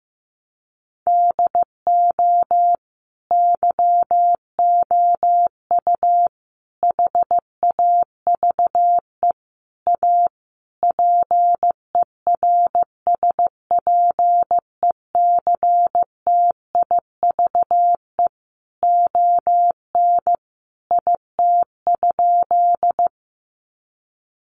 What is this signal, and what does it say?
Morse code, heard as a keyed tone, about 15 wpm, 700 Hz. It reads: DO YOU HAVE A PERSPECTIVE ON IT?
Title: Sentences from Top 1200 Words 15wpm